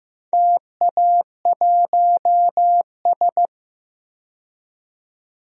Morse code, heard as a keyed tone, about 15 wpm, 700 Hz. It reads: TA1S
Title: DXCC-100 Call Signs - 4 Characters - 15wpm